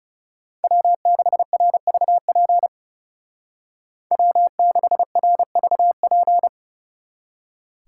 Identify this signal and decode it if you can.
Morse code, heard as a keyed tone, about 37 wpm, 700 Hz. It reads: W6RVP W6RVP
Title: Speed Racing - USA Call Signs 30wpm